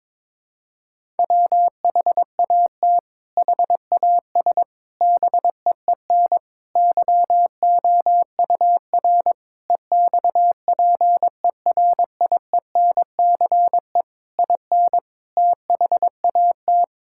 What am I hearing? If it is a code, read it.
Morse code, heard as a keyed tone, about 22 wpm, 700 Hz. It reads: WHAT HAS BEEN YOUR EXPERIENCE IN THAT
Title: Sentences from Top 1000 Words - Review Part 6 - 22wpm